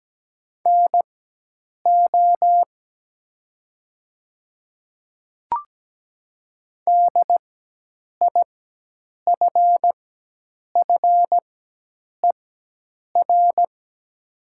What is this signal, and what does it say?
Morse code, heard as a keyed tone, about 17 wpm, spaced as for 8 wpm, 700 Hz. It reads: NO DIFFER
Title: 4x Character Spacing - Top 200 Words - 17wpm